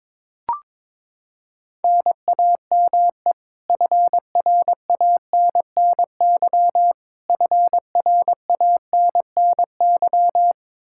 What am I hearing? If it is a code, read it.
Morse code, heard as a keyed tone, about 22 wpm, 700 Hz. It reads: NAME FRANNY FRANNY
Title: Practice QSO - Exchange 1 - Beginning - 22wpm